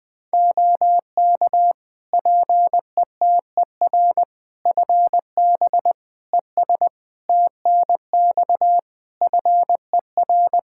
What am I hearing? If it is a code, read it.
Morse code, heard as a keyed tone, about 20 wpm, 700 Hz. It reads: OK PETER FB ES TNX FER
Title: K1USN SST QSO Practice 20wpm